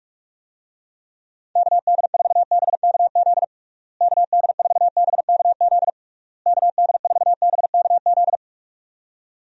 Morse code, heard as a keyed tone, about 46 wpm, 700 Hz. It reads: KD4BKZ KD4BKZ KD4BKZ
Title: USA Call Signs - 6 Characters - Repeated 3x - 45wpm